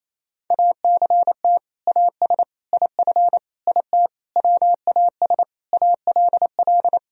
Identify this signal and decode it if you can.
Morse code, heard as a keyed tone, about 28 wpm, 700 Hz. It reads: ACT AS IF IT WAS ALL
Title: Sentences from Top 1000 Words - Review Part 1 - 28wpm